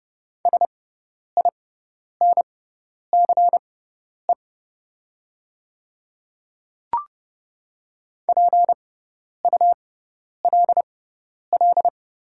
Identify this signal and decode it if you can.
Morse code, heard as a keyed tone, about 31 wpm, spaced as for 10 wpm, 700 Hz. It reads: SINCE PULL
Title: 6x Character Spacing - Top 500 Words - 30wpm